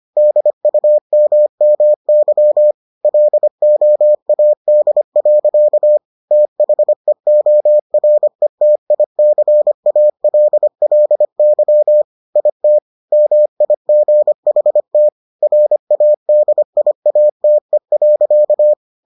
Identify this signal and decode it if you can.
Morse code, heard as a keyed tone, about 25 wpm, 600 Hz. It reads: DUMMY LOAD. THEORETICALLY IT MIGHT RADIATE.